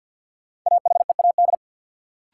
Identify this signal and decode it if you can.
Morse code, heard as a keyed tone, about 50 wpm, 700 Hz. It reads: AHEAD
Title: Sentences from Top 1100 Words - New Word Drill Alt - 50wpm